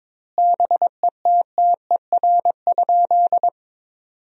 Morse code, heard as a keyed tone, about 22 wpm, 700 Hz. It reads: BETTER?
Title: 2x Word-Spacing - Sentences from Top 400 Words - 22wpm